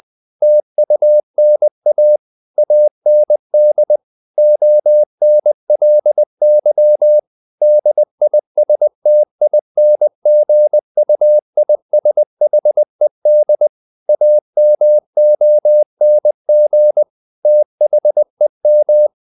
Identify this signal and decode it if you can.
Morse code, heard as a keyed tone, about 20 wpm, 600 Hz. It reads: TUNA AND ONLY DISTINGUISHED AMONG THEM
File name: the old man and the sea / the old man and the sea part 1 20wpm